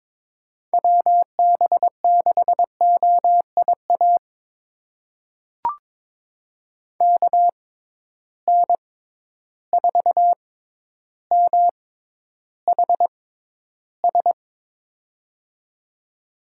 Morse code, heard as a keyed tone, about 22 wpm, spaced as for 11 wpm, 700 Hz. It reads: WB6OIA KN4MHS
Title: USA Call Signs - 6 Characters - 6x Character Spacing - 22wpm